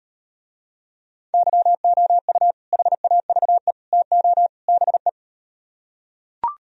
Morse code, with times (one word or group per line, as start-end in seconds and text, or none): 1.3-2.5 YOU
2.7-3.7 HAVE
3.9-4.5 TO
4.7-5.1 BE